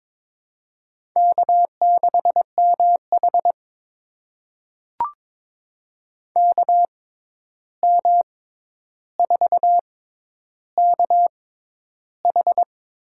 Morse code, heard as a keyed tone, about 21 wpm, spaced as for 10 wpm, 700 Hz. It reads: K6MH KM4KH
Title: USA Call Signs - 6x Character Spacing - 22wpm